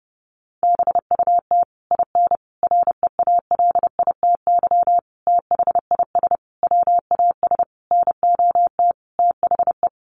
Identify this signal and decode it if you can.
Morse code, heard as a keyed tone, about 29 wpm, 700 Hz. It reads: BUT IN REALITY THIS WAS NOT THE
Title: Sentences from Top 1200 Words 30wpm